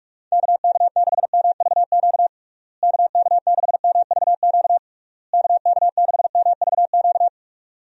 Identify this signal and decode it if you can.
Morse code, heard as a keyed tone, about 41 wpm, 700 Hz. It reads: KK6MVQ KK6MVQ KK6MVQ
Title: USA Call Signs - 6 Characters - Repeated 3x - 45wpm